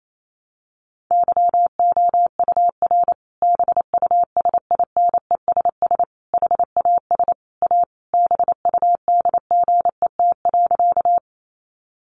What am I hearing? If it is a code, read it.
Morse code, heard as a keyed tone, about 28 wpm, 700 Hz. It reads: YOUR BUSINESS HAS A BUDGET.
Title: Sentences from Top 1200 Words 28wpm